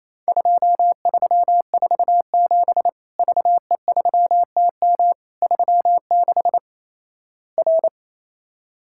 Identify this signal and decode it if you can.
Morse code, heard as a keyed tone, about 28 wpm, 700 Hz. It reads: 2347 VE3TM 36 R
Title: Worked All Europe DX Contest - QTC Exchange - 28wpm